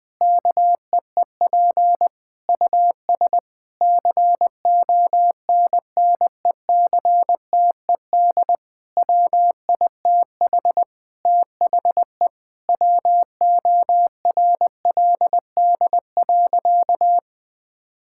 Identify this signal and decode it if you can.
Morse code, heard as a keyed tone, about 20 wpm, 700 Hz. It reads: KEEP US CONNECTED WITH THE WORLD.
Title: Difficult Sentences with Hints - 20wpm